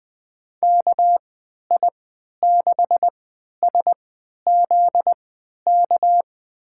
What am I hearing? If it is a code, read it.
Morse code, heard as a keyed tone, about 20 wpm, spaced as for 11 wpm, 700 Hz. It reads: KI6SZK
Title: USA Call Signs - 6 Characters - 3x Character Spacing - 20wpm